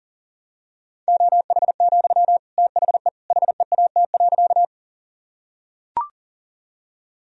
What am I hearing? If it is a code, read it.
Morse code, heard as a keyed tone, about 40 wpm, 700 Hz. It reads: OH, THE HEAT.
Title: Sentences from Top 1000 Words - Review Part 3 - 40wpm